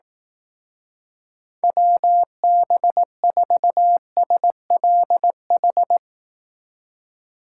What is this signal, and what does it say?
Morse code, heard as a keyed tone, about 19 wpm, 700 Hz. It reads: WB4SLH
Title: Speed Racing - USA 2x3 Call Signs 15wpm